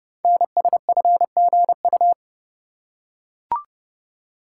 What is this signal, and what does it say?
Morse code, heard as a keyed tone, about 30 wpm, 700 Hz. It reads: NSFGU
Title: Code Groups - 5 Letters - 30wpm